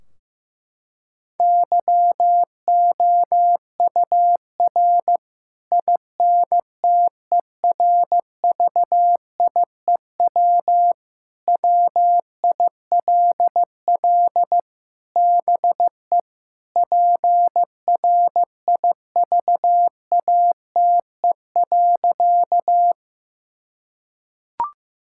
Morse code, heard as a keyed tone, about 15 wpm, 700 Hz. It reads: YOUR INTERVIEW WILL BE PRIVATE.
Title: Sentences from Top 1600 Words 15wpm